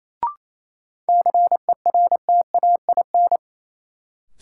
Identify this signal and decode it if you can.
Morse code, heard as a keyed tone, about 28 wpm, 700 Hz. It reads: CERTAIN
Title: Sentences from Top 500 Words - New Word Drill - 28wpm